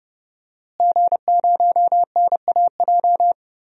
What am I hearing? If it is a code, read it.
Morse code, heard as a keyed tone, about 30 wpm, 700 Hz. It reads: G0NAJ
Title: DXCC-100 Call Signs - 5 Characters - Repeated 2x - 30wpm